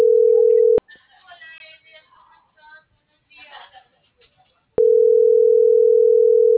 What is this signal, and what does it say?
Ringback tone (North America), 440 + 480 Hz.